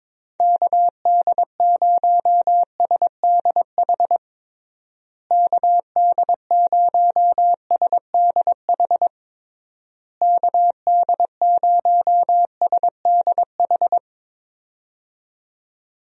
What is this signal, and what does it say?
Morse code, heard as a keyed tone, about 22 wpm, 700 Hz. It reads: KD0SDH KD0SDH KD0SDH
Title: USA Call Signs - 6 Characters - Repeated 3x - 22wpm